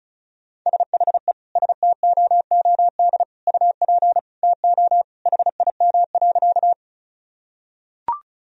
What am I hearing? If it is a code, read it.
Morse code, heard as a keyed tone, about 36 wpm, 700 Hz. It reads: SHE STOOD UP TO HIM.
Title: Sentences from Top 1000 Words - Review Part 2 - 35wpm